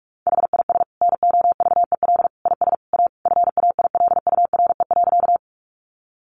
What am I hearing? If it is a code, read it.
Morse code, heard as a keyed tone, about 46 wpm, 700 Hz. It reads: HIS NOVEL IS A FAILURE.
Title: Sentences from Top 1300 Words 45wpm